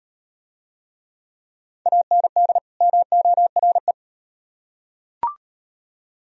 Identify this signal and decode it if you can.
Morse code, heard as a keyed tone, about 38 wpm, 700 Hz. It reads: AND MORE